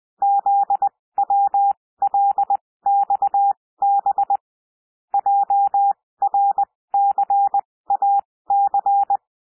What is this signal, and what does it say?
Morse code, heard as a keyed tone, about 20 wpm, spaced as for 16 wpm, 800 Hz. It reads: ZWLXB JRCAC
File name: cwmp3 / L1 SOUND LETTERS 20 wpm